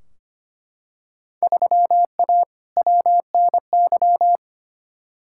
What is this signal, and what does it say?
Morse code, heard as a keyed tone, about 25 wpm, 700 Hz. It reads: 3A WNY
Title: Field Day Exchanges - 25wpm